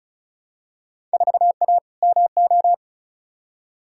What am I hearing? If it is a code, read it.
Morse code, heard as a keyed tone, about 36 wpm, 700 Hz. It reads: 4A MO
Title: Field Day Exchanges - 35wpm